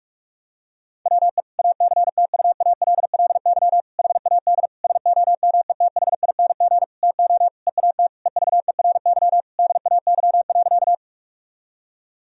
Morse code, heard as a keyed tone, about 41 wpm, 700 Hz. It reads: WE ACTUALLY HAD SOMETHING TO EAT EVERY DAY.
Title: Sentences from Top 1000 Words - Review Part 4 - 45wpm